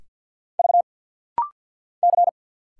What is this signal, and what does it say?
Morse code, heard as a keyed tone, about 50 wpm, 700 Hz. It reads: V C
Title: Single Letters - Rapid-Fire 50wpm